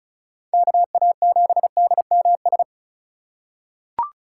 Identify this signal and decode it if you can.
Morse code, heard as a keyed tone, about 34 wpm, 700 Hz. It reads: KA7DMS
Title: USA Call Signs - 6 Characters - 2x Character Spacing - 35wpm